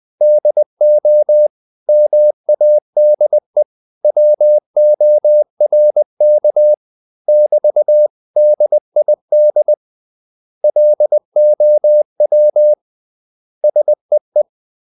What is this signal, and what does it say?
Morse code, heard as a keyed tone, about 20 wpm, 600 Hz. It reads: DO MADE WORK <BT>DID LOW SEE
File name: CW / Practice / 500MostCommonEnglishWords20